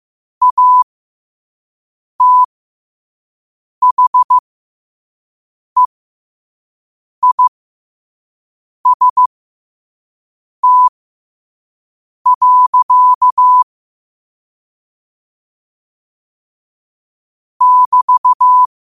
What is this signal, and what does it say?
Morse code, heard as a keyed tone, about 15 wpm, spaced as for 5 wpm, 1000 Hz. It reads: ATHEIST. <BT>
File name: qotdfiles / 5wpm